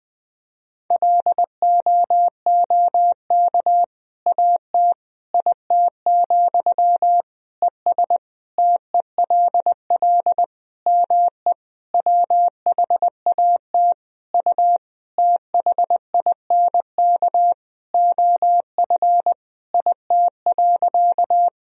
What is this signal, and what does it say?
Morse code, heard as a keyed tone, about 21 wpm, 700 Hz. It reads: LOOK AT IT, ES TELL ME WHAT U THINK OF IT.
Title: Sentences from Top 200 Words and Abbreviations 20wpm